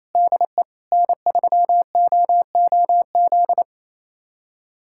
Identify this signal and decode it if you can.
Morse code, heard as a keyed tone, about 28 wpm, 700 Hz. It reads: DE N3OOZ
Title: USA Callsign DE Callsign 28wpm